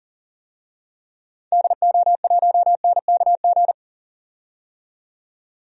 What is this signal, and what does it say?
Morse code, heard as a keyed tone, about 40 wpm, 700 Hz. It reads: DO1NKG
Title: DXCC-100 Call Signs - 6 Characters - 40wpm